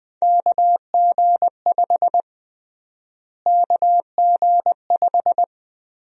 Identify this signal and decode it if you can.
Morse code, heard as a keyed tone, about 21 wpm, 700 Hz. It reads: KG5 KG5
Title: USA Call Sign Prefixes - Repeated 2x - 20wpm